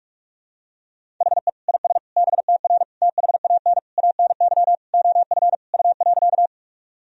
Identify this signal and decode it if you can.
Morse code, heard as a keyed tone, about 41 wpm, 700 Hz. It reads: HE IS BTR THAN ANY OF U.
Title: Sentences from Top 400 Words and Abbreviations 45wpm